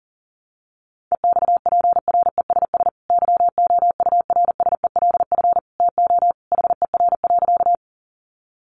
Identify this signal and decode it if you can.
Morse code, heard as a keyed tone, about 40 wpm, 700 Hz. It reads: EXPRESS YOURSELF TO HER.
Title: Sentences from Top 1100 Words 40wpm